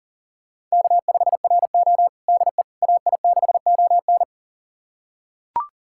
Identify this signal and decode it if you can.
Morse code, heard as a keyed tone, about 40 wpm, 700 Hz. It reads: K5RO DE AI6ON
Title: USA Callsign DE Callsign 40wpm